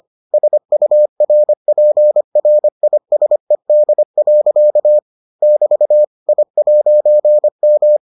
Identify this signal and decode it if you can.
Morse code, heard as a keyed tone, about 25 wpm, 600 Hz. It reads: SURPRISED. <BT> I'M